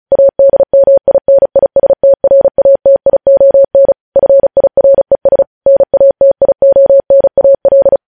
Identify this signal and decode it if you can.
Morse code, heard as a keyed tone, about 36 wpm, 550 Hz. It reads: ADMINISTRATION FIRES NATIONAL
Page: news headlines in morse code at 35 wpm